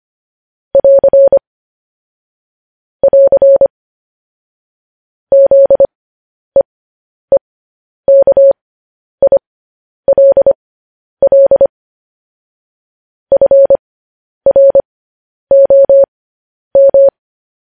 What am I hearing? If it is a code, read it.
Morse code, heard as a keyed tone, about 25 wpm, spaced as for 10 wpm, 550 Hz. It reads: <AR> <AR> ZEEKILL FROM